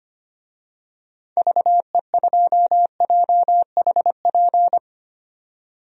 Morse code, heard as a keyed tone, about 25 wpm, 690 Hz. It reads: VE2JHP
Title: DXCC-100 Call Signs - 6 Characters - 25wpm